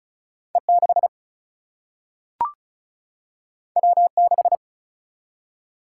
Morse code, heard as a keyed tone, about 36 wpm, 700 Hz. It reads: E 6 W 6
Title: Single-Letter Single-Number 35wpm